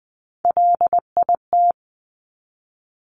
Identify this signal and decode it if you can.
Morse code, heard as a keyed tone, about 20 wpm, 700 Hz. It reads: LIT
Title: Top Three Letter Combinations in Words 20wpm